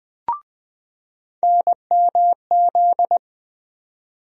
Morse code, heard as a keyed tone, about 21 wpm, 700 Hz. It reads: NMZ
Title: USA Call Sign Postfixes - 3 Letters 20wpm